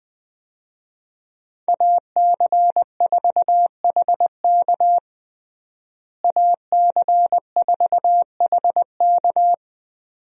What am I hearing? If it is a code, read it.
Morse code, heard as a keyed tone, about 20 wpm, 700 Hz. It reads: AC4HK AC4HK
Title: USA Call Signs - 5 Characters - Repeated 2x - 20wpm